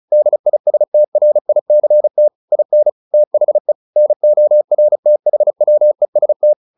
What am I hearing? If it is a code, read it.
Morse code, heard as a keyed tone, about 35 wpm, 600 Hz. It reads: DISTRICT IN THE NORTHWEST